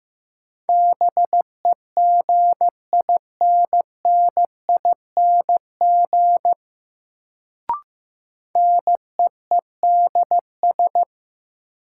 Morse code, heard as a keyed tone, about 15 wpm, 700 Hz. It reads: BEGINNING NEEDS
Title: Sets of 3 Words - New Word Drill - 15wpm